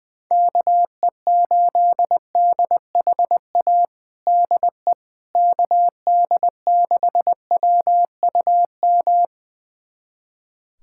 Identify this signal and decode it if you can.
Morse code, heard as a keyed tone, about 20 wpm, 700 Hz. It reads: KE8DHA DE KD6WUM
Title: USA Callsign DE Callsign - 6 Characters - 20wpm